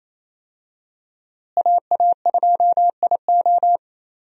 Morse code, heard as a keyed tone, about 28 wpm, 700 Hz. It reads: AA2IO
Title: USA Call Signs - 5 Characters - Repeated 3x - 28wpm